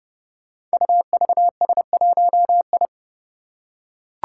Morse code, heard as a keyed tone, about 30 wpm, 700 Hz. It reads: UVS1I
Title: Five Challenging Letter-Number Combinations 30wpm